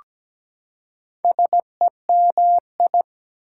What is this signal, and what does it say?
Morse code, heard as a keyed tone, about 17 wpm, 700 Hz. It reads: SEMI
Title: Prefixes and Suffixes 17wpm